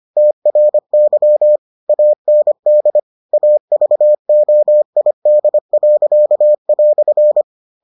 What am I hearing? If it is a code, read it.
Morse code, heard as a keyed tone, about 25 wpm, 600 Hz. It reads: TRY AND AVOID."